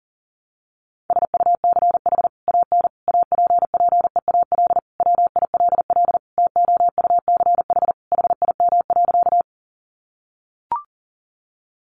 Morse code, heard as a keyed tone, about 40 wpm, 700 Hz. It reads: SUCH AN APPEAL WILL TOUCH HIM.
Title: Sentences from Top 1300 Words 40wpm